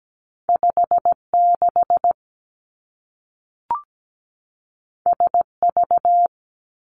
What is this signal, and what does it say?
Morse code, heard as a keyed tone, about 17 wpm, 700 Hz. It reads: 56 SV
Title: Two Challenging Letter-Number Combinations 17wpm